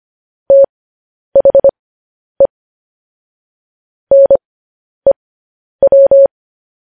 Morse code, heard as a keyed tone, about 25 wpm, spaced as for 10 wpm, 550 Hz. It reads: THE NEW
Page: News Headlines in Morse Code at 10 WPM